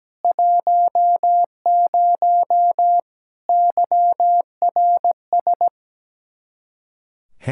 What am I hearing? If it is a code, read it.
Morse code, heard as a keyed tone, about 17 wpm, 700 Hz. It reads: 10 YRS
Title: Practice QSO - Exchange 3 - More Info - 17wpm